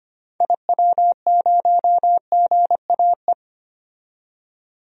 Morse code, heard as a keyed tone, about 25 wpm, 700 Hz. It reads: IW0GAE
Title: DXCC-100 Call Signs - 6 Characters - 25wpm